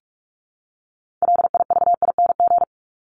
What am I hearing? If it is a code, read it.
Morse code, heard as a keyed tone, about 44 wpm, 700 Hz. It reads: LIVING
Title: ING Suffix - Top 100 Words with Base Words 45wpm